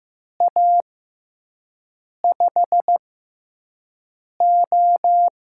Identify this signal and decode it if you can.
Morse code, heard as a keyed tone, about 15 wpm, spaced as for 5 wpm, 700 Hz. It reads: A5O
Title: USA Call Signs - 6 Characters - 6x Character Spacing - 15wpm